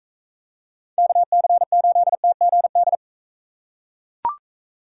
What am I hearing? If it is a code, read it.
Morse code, heard as a keyed tone, about 42 wpm, 700 Hz. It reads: KC8TGD